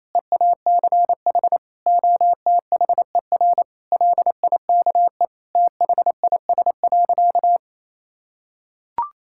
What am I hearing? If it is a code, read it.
Morse code, heard as a keyed tone, about 28 wpm, 700 Hz. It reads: EACH OTHER LIKE THIS.
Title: Sentences from Top 1000 Words - Review Part 6 - 28wpm